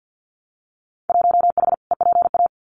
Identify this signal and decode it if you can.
Morse code, heard as a keyed tone, about 50 wpm, 700 Hz. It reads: 1H EPA
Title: Winter Field Day Exchanges 50wpm